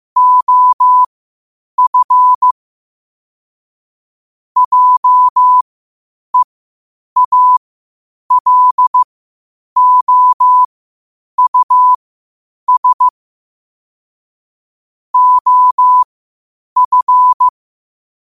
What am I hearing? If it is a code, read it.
Morse code, heard as a keyed tone, about 15 wpm, spaced as for 8 wpm, 1000 Hz. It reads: OF JEALOUS OF
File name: qotdfiles / 8wpm